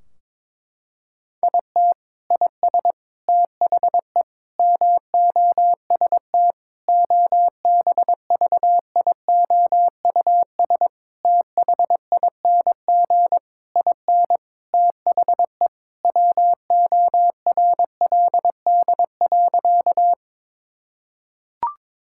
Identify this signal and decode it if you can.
Morse code, heard as a keyed tone, about 22 wpm, 700 Hz. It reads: IT IS THE MOST OBVIOUS THING IN THE WORLD.
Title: Sentences from Top 1600 Words 22wpm